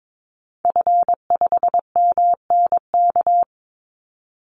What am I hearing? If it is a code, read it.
Morse code, heard as a keyed tone, about 22 wpm, 700 Hz. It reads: F5MNK
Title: Call Signs - 22wpm